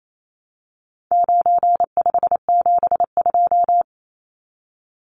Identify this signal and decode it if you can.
Morse code, heard as a keyed tone, about 28 wpm, 700 Hz. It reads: 9572
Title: Numbers - 4-Digit 28wpm